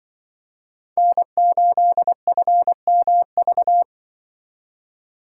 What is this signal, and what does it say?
Morse code, heard as a keyed tone, about 25 wpm, 700 Hz. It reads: N8FMV
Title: Speed Racing - USA 1x3 Call Signs 20wpm